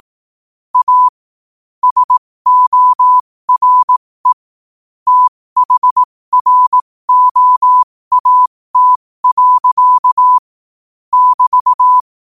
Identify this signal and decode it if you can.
Morse code, heard as a keyed tone, about 18 wpm, spaced as for 15 wpm, 1000 Hz. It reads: A SORE THROAT. <BT>
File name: qotdfiles / 15wpm